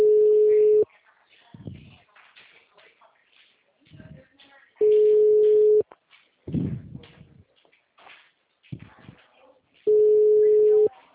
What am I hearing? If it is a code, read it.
Ringback tone (Europe, 425 Hz), with speech.
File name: out- -2002- - - .75